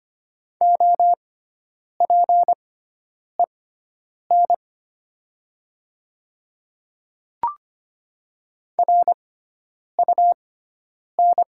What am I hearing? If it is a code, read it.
Morse code, heard as a keyed tone, about 25 wpm, spaced as for 9 wpm, 700 Hz. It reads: OPEN RUN
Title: 6x Character Spacing - Top 300 Words - 25wpm